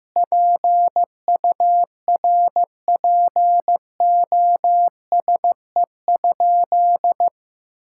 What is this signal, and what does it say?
Morse code, heard as a keyed tone, about 15 wpm, 700 Hz. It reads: PURPOSE?
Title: Sentences from Top 1000 Words - Review Part 5 - 15wpm